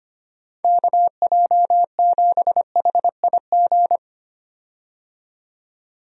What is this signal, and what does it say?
Morse code, heard as a keyed tone, about 25 wpm, 700 Hz. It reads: KJ7HIG
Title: USA Call Signs - 6 Characters - 25wpm